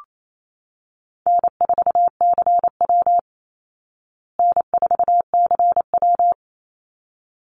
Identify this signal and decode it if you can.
Morse code, heard as a keyed tone, about 28 wpm, 700 Hz. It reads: N4CW N4CW
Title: Easy Call Signs Repeated 2x - 28wpm